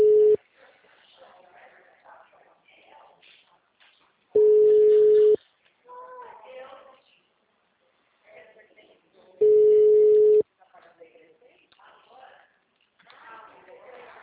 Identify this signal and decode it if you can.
Ringback tone (Europe, 425 Hz), with speech.